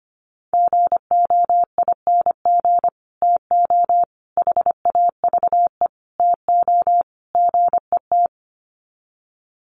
Morse code, heard as a keyed tone, about 25 wpm, 700 Hz. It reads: GOING TO HAVE TO GET
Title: Sets of 5 Words Encore2 25wpm